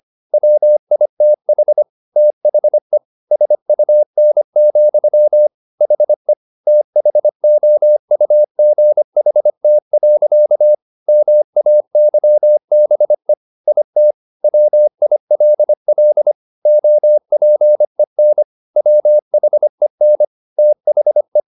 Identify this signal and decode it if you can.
Morse code, heard as a keyed tone, about 25 wpm, 600 Hz. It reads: WITH THE SUN, HE THOUGHT. MAYBE IT WILL OPEN WHEN THE
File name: the old man and the sea part 1 25wpm